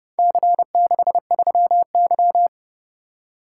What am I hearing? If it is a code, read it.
Morse code, heard as a keyed tone, about 29 wpm, 700 Hz. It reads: C63Y